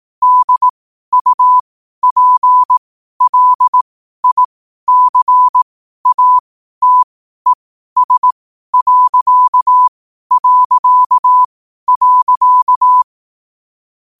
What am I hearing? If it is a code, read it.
Morse code, heard as a keyed tone, about 18 wpm, spaced as for 12 wpm, 1000 Hz. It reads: DUPLICATES...
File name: qotdfiles / 12wpm